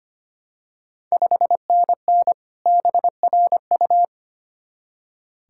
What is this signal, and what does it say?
Morse code, heard as a keyed tone, about 25 wpm, 700 Hz. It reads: 5NN BRU
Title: Ontario QSO Party Exchange 25wpm